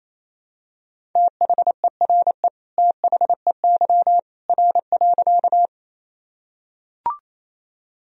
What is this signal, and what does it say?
Morse code, heard as a keyed tone, about 27 wpm, 700 Hz. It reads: THERE THEY R.
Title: Sentences from Top 100 Words and Abbreviations 28wpm